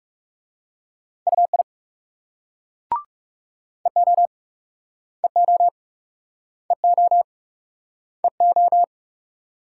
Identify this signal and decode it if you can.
Morse code, heard as a keyed tone, about 38 wpm, 700 Hz. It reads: UI EO EO EO EO